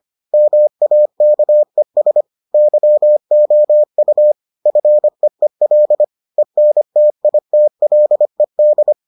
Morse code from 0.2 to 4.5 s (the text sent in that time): MAKES YOU